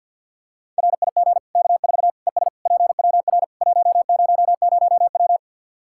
Morse code, heard as a keyed tone, about 49 wpm, 700 Hz. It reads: RIG K4 ES PWR 100W
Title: Practice QSO - Exchange 2 - Info - 50wpm